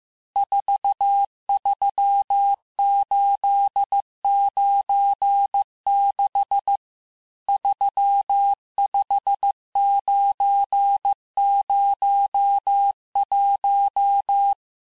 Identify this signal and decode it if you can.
Morse code, heard as a keyed tone, about 15 wpm, 800 Hz. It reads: 43896 35901
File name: cours-du-lundi files / cours-1585-texte-chiffre-vitesse-14mpm